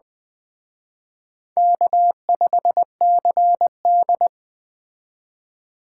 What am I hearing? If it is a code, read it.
Morse code, heard as a keyed tone, about 20 wpm, 700 Hz. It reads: K5CD
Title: Speed Racing - USA 1x2 Call Signs 17wpm